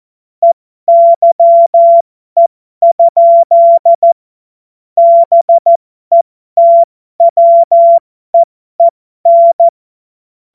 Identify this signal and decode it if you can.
Morse code, heard as a keyed tone, about 14 wpm, spaced as for 12 wpm, 680 Hz. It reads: EYE? BETWEEN
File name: Morsekurs / Witze